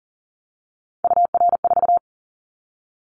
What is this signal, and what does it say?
Morse code, heard as a keyed tone, about 40 wpm, 700 Hz. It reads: UR4
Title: Callsign Prefixes for Top 25 DXCC - 40wpm